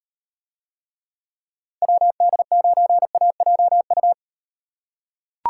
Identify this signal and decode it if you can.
Morse code, heard as a keyed tone, about 38 wpm, 700 Hz. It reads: WD9AJU